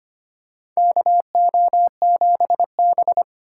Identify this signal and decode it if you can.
Morse code, heard as a keyed tone, about 25 wpm, 700 Hz. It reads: KO7B
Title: USA Call Signs - 4 Characters - 2x Character Spacing - 25wpm